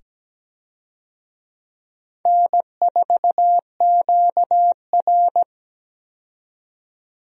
Morse code, heard as a keyed tone, about 17 wpm, 700 Hz. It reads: N4QR
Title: USA Call Signs - 4 Characters - 17wpm